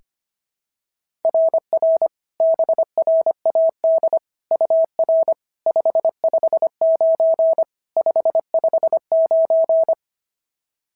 Morse code, heard as a keyed tone, about 22 wpm, 650 Hz. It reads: RR BRAD UR 559 559
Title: SOTA QSO Practice 25wpm